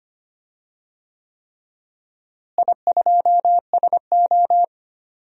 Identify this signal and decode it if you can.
Morse code, heard as a keyed tone, about 25 wpm, 700 Hz. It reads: I2SO
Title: DXCC-100 Call Signs - 4 Characters - Repeated 3x - 25wpm